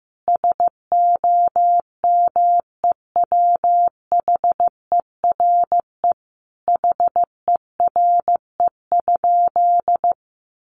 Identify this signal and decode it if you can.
Morse code, heard as a keyed tone, about 15 wpm, 700 Hz. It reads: SOMEWHERE HERE?
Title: Sentences from Top 1400 Words 15wpm